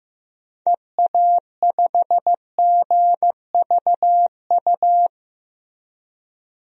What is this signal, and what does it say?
Morse code, heard as a keyed tone, about 15 wpm, 700 Hz. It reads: EA5GVU
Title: DXCC-100 Call Signs - 6 Characters - 15wpm